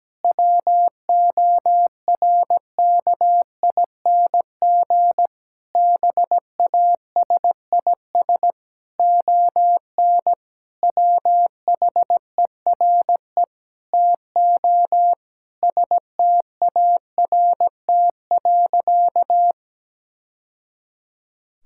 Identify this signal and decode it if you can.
Morse code, heard as a keyed tone, about 17 wpm, 700 Hz. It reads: WORKING BASIS ON WHERE TO START.
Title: Sentences from Top 1000 Words - Review Part 5 - 17wpm